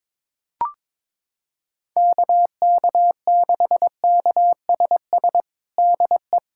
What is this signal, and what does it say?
Morse code, heard as a keyed tone, about 22 wpm, 700 Hz. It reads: KK6KSS DE
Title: USA Callsign DE Callsign - 6 Characters - 22wpm